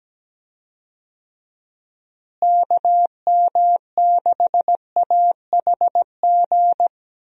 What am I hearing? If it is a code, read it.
Morse code, heard as a keyed tone, about 17 wpm, 700 Hz. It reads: KM6AHG
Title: USA Call Signs - 6 Characters - Repeated 2x - 17wpm